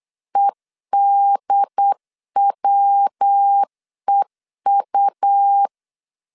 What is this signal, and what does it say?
Morse code, heard as a keyed tone, about 8 wpm, 800 Hz. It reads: EDWEU
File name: CW / DET / 1 3 1500C S B 8wpm 4141